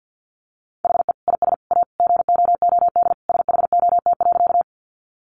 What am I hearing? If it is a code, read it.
Morse code, heard as a keyed tone, about 50 wpm, 700 Hz. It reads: HE IS A GOOD SHOT.
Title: Sentences from Top 1100 Words 50wpm